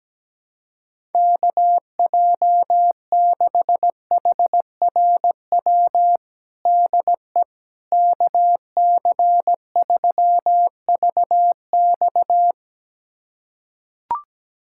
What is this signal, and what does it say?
Morse code, heard as a keyed tone, about 17 wpm, 700 Hz. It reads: KJ6HRW DE KC3VX
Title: USA Callsign DE Callsign 17wpm